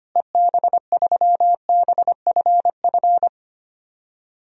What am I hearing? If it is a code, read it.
Morse code, heard as a keyed tone, about 25 wpm, 700 Hz. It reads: EB3BFF
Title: DXCC-100 Call Signs - 6 Characters - Repeated 3x - 25wpm